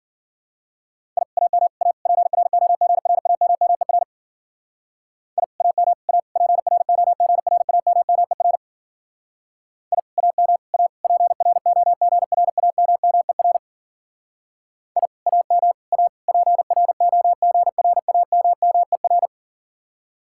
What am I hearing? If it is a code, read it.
Morse code, heard as a keyed tone, about 49 wpm, 700 Hz. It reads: I AM A PROGRAMMER I AM A PROGRAMMER I AM A PROGRAMMER I AM A PROGRAMMER
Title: Speed-Racing Common QSO Phrases Encore - 40wpm